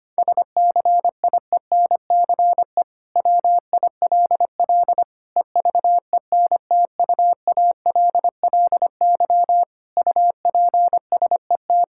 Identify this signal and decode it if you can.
Morse code, heard as a keyed tone, about 25 wpm, 700 Hz. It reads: SCIENCE WILL EVENTUALLY UPSET
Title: Sentences from Top 1000 Words - Review Part 6 - 25wpm